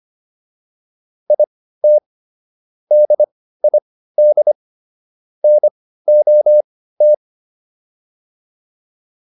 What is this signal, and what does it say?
Morse code, heard as a keyed tone, about 25 wpm, spaced as for 15 wpm, 600 Hz. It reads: IT DID NOT